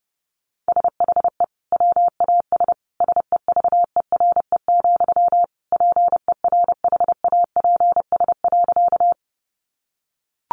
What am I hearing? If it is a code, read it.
Morse code, heard as a keyed tone, about 30 wpm, 700 Hz. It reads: SHE WAS SEVERE, PERHAPS.
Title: Sentences from Top 1400 Words 30wpm